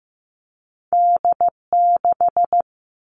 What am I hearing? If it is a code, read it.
Morse code, heard as a keyed tone, about 15 wpm, 700 Hz. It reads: D6
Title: All Cut Numbers - 2-Digit 15wpm